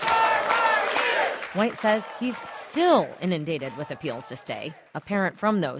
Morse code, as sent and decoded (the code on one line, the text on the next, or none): none